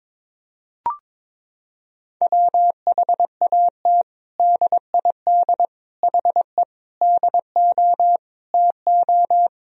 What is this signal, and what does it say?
Morse code, heard as a keyed tone, about 21 wpm, 700 Hz. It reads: WHAT DID HE DO TO
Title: Sentences from Top 100 Words and Abbreviations 22wpm